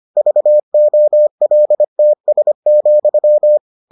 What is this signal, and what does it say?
Morse code, heard as a keyed tone, about 25 wpm, 600 Hz. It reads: VOLTS,